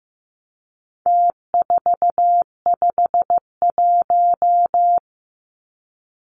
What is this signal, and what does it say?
Morse code, heard as a keyed tone, about 15 wpm, 700 Hz. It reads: T451
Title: Numbers - Contest Style - 4-Digit 15wpm